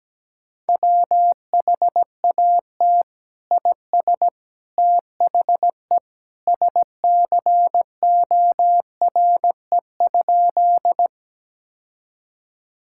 Morse code, heard as a keyed tone, about 17 wpm, 700 Hz. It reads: WHAT IS THE SCORE?
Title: Sentences from Top 1600 Words 17wpm